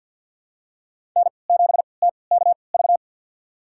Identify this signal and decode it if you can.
Morse code, heard as a keyed tone, about 52 wpm, spaced as for 29 wpm, 700 Hz. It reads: N7TKV
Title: USA Call Signs - 3x Character Spacing - 50wpm